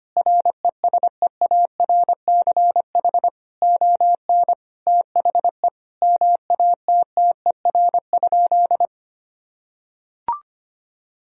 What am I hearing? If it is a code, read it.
Morse code, heard as a keyed tone, about 25 wpm, 700 Hz. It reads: RESEARCH ON THE MATTER?
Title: Sentences from Top 1000 Words - Review Part 4 - 25wpm